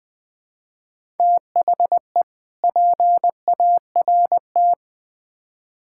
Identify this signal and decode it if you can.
Morse code, heard as a keyed tone, about 20 wpm, 700 Hz. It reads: THE PART
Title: Build A Phrase - 4-Word Phrases - 20wpm